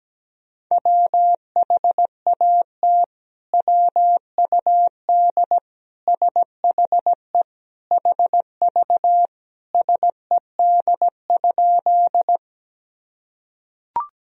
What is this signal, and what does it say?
Morse code, heard as a keyed tone, about 17 wpm, 700 Hz. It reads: WHAT WUD SHE HV SED?
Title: Sentences from Top 100 Words and Abbreviations 17wpm